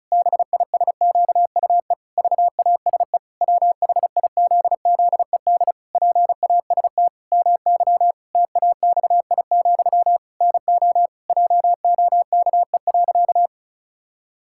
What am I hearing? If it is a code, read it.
Morse code, heard as a keyed tone, about 36 wpm, 700 Hz. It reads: BISQUE VASE WHIZZED PAST MY TAXI, NO JOKE.
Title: Pangram Challenge v2 - 35wpm